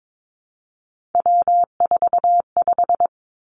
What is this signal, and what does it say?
Morse code, heard as a keyed tone, about 22 wpm, 700 Hz. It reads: W45
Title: Three Challenging Letter-Number Combinations 22wpm